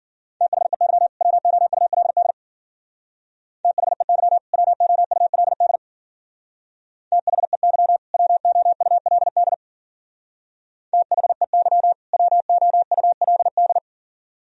Morse code, heard as a keyed tone, about 52 wpm, 700 Hz. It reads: THEY WOULD THEY WOULD THEY WOULD THEY WOULD